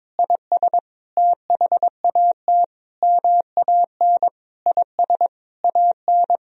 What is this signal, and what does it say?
Morse code, heard as a keyed tone, about 22 wpm, 700 Hz. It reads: IS THAT MAN IS AN
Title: Sentences from Top 1000 Words - Review Part 1 - 22wpm